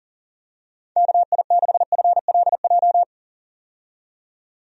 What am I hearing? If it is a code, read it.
Morse code, heard as a keyed tone, about 40 wpm, 700 Hz. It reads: KI6FLJ
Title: USA Call Signs - 6 Characters - 40wpm